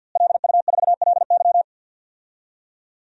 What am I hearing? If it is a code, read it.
Morse code, heard as a keyed tone, about 46 wpm, 700 Hz. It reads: LU4LY
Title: DXCC-100 Call Signs - 5 Characters - Repeated 3x - 50wpm